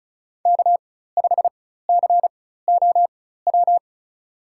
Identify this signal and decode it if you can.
Morse code, heard as a keyed tone, about 34 wpm, spaced as for 16 wpm, 700 Hz. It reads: K5COW
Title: USA Call Signs - 5 Characters - 4x Character Spacing - 35wpm